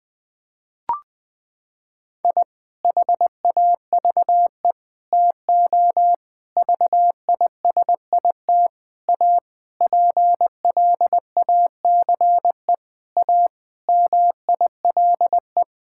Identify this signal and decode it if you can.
Morse code, heard as a keyed tone, about 20 wpm, 700 Hz. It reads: I HAVE TO VISIT A PLACE A MILE
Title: Sentences from Top 1000 Words - Review Part 6 - 20wpm